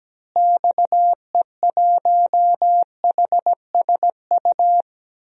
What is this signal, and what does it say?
Morse code, heard as a keyed tone, about 17 wpm, 700 Hz. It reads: XE1HSU
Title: DXCC-100 Call Signs - 6 Characters - Repeated 3x - 17wpm